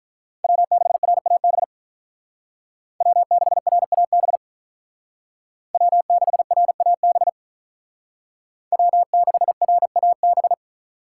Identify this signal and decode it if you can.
Morse code, heard as a keyed tone, about 41 wpm, 700 Hz. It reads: W6RAB W6RAB W6RAB W6RAB
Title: Speed Racing - USA Call Signs 35wpm